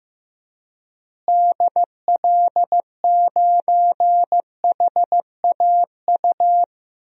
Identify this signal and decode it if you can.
Morse code, heard as a keyed tone, about 15 wpm, 700 Hz. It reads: DL9HAU
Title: DXCC-100 Call Signs - 6 Characters - Repeated 3x - 15wpm